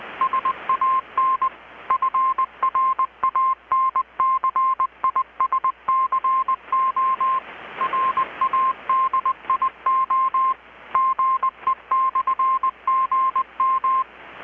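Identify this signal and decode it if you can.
Morse code, heard as a keyed tone, about 20 wpm, 1050 Hz. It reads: SAN FRANCISCO RADIO GE/GM
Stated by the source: Morse from KPH's Night of Nights XX maritime radiotelegraphy event, received in Chiapas, Mexico about 3,800 km away on a Kenwood R-600 with a grounded 10 m long-wire antenna.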